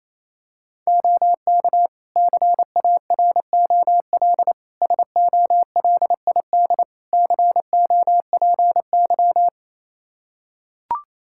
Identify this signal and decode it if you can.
Morse code, heard as a keyed tone, about 28 wpm, 700 Hz. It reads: OK CAROL SOLID COPY
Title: K1USN SST QSO Practice 28wpm